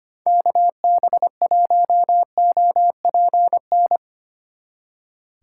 Morse code, heard as a keyed tone, about 32 wpm, 700 Hz. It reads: KB1OPN